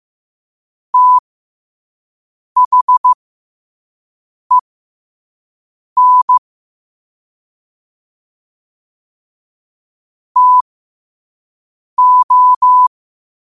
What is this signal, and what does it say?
Morse code, heard as a keyed tone, about 15 wpm, spaced as for 5 wpm, 1000 Hz. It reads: THEN TO
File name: qotdfiles / 5wpm